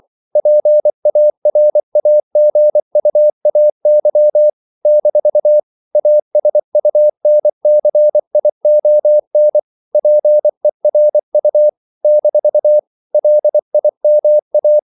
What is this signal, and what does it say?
Morse code, heard as a keyed tone, about 24 wpm, 600 Hz. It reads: PARAGUAY - ASUNCION PERU - LIMA
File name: CW / Practice / NationalCapitals24